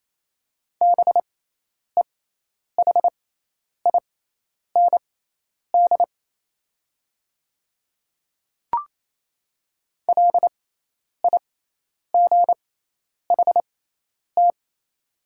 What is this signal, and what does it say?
Morse code, heard as a keyed tone, about 28 wpm, spaced as for 10 wpm, 700 Hz. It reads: BEHIND LIGHT
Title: 6x Character Spacing - Top 500 Words - 28wpm